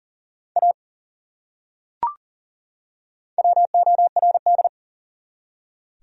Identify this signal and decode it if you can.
Morse code, heard as a keyed tone, about 40 wpm, 700 Hz. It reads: A WORD
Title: Top 100 Words 40wpm